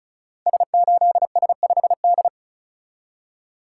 Morse code, heard as a keyed tone, about 36 wpm, 700 Hz. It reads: S8S5D
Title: Code Groups - 5 Letter:Numbers - 35wpm